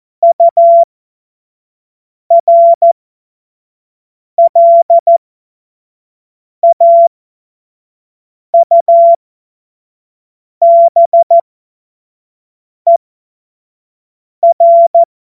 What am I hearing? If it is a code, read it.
Morse code, heard as a keyed tone, about 14 wpm, spaced as for 5 wpm, 680 Hz. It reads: URLAUBER
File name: Morsekurs / Witze / Witz 07. Februar 2023 25 F70